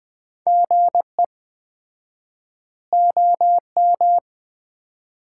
Morse code, heard as a keyed tone, about 20 wpm, 700 Hz. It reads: GE OM
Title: Common QSO Phrases - 4x Word-Spacing - 20wpm